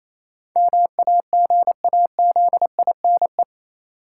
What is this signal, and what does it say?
Morse code, heard as a keyed tone, about 28 wpm, 700 Hz. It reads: MAGAZINE
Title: Sentences from Top 1000 Words - Review Part 7 - New Word Drill - 28wpm